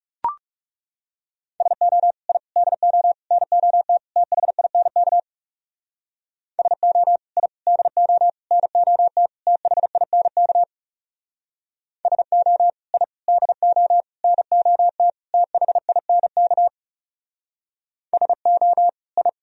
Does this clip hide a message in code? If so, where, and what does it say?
Yes: Morse code from 1.6 to 19.3 s, heard as a keyed tone, about 40 wpm, 700 Hz. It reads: SO I DO NOT THINK SO I DO NOT THINK SO I DO NOT THINK SO I